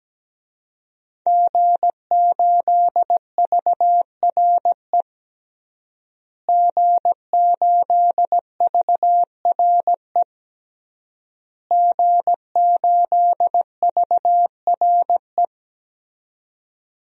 Morse code, heard as a keyed tone, about 17 wpm, 700 Hz. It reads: G8VRE G8VRE G8VRE
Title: DXCC-100 Call Signs - 5 Characters - Repeated 3x - 17wpm